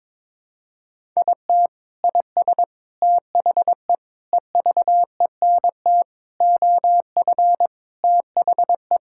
Morse code, heard as a keyed tone, about 22 wpm, 700 Hz. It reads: IT IS THE EVENT OF THE
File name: practice / 22wpm / Sentences with 3-Word Phrases - 22wpm